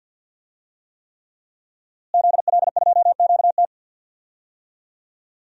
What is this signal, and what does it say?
Morse code, heard as a keyed tone, about 45 wpm, 700 Hz. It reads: ZL2QT
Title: DXCC-100 Call Signs - 5 Characters - Repeated 3x - 50wpm